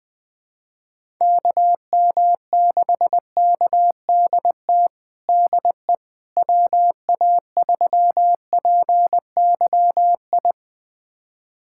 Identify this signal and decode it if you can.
Morse code, heard as a keyed tone, about 20 wpm, 700 Hz. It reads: KM6KDT DE WA3PYI
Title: USA Callsign DE Callsign - 6 Characters - 20wpm